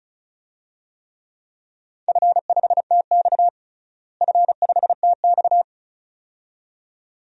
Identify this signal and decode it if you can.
Morse code, heard as a keyed tone, about 36 wpm, 700 Hz. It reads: F5TX F5TX
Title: DXCC-100 Call Signs - 4 Characters - Repeated 2x - 35wpm